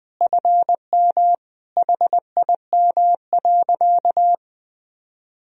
Morse code, heard as a keyed tone, about 20 wpm, 700 Hz. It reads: FM HIM.
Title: Sentences from Top 400 Words and Abbreviations 20wpm